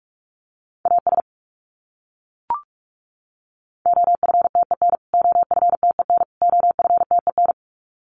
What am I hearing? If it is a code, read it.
Morse code, heard as a keyed tone, about 44 wpm, 700 Hz. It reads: AS OFTEN OFTEN OFTEN
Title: Top 500 Words Repeated 45wpm